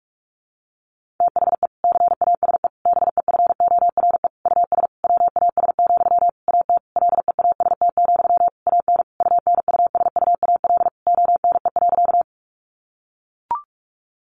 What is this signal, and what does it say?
Morse code, heard as a keyed tone, about 44 wpm, 700 Hz. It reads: THE CASE BEFORE US WAS, AT LEAST, AN UNUSUAL ONE.
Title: Sentences from Top 1400 Words 45wpm